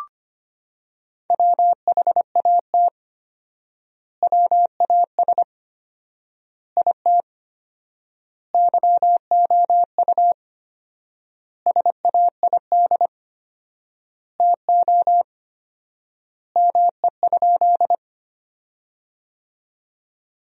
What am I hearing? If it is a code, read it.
Morse code, heard as a keyed tone, about 25 wpm, 700 Hz. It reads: WHAT WAS IT YOU SAID TO ME?
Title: 4x Word-Spacing - Sentences from Top 200 Words - 25wpm